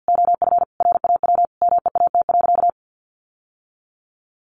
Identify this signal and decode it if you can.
Morse code, heard as a keyed tone, about 50 wpm, 700 Hz. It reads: OF RAW MEAT.
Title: Sentences from Top 1500 Words 50wpm